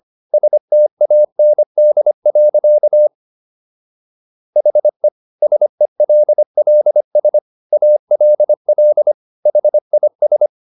Morse code, heard as a keyed tone, about 25 wpm, 600 Hz. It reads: STAND. HE SELLS ALL HIS